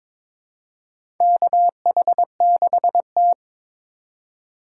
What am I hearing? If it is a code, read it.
Morse code, heard as a keyed tone, about 22 wpm, 700 Hz. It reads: KH6T
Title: USA Call Signs - 4 Characters - 3x Character Spacing - 22wpm